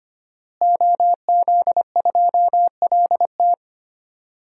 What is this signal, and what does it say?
Morse code, heard as a keyed tone, about 25 wpm, 700 Hz. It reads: OZ2LT
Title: DXCC-100 Call Signs - 5 Characters - Repeated 2x - 25wpm